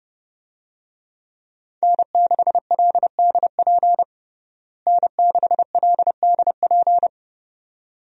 Morse code, heard as a keyed tone, about 32 wpm, 700 Hz. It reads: N6LDP N6LDP
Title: USA Call Signs - 5 Characters - Repeated 2x - 30wpm